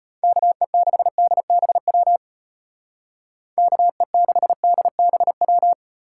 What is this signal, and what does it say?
Morse code, heard as a keyed tone, about 32 wpm, 700 Hz. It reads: KE6DBW KE6DBW